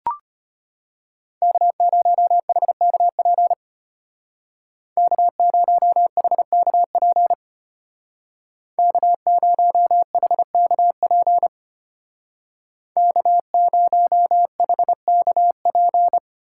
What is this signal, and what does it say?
Morse code, heard as a keyed tone, about 33 wpm, 700 Hz. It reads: K0HKP K0HKP K0HKP K0HKP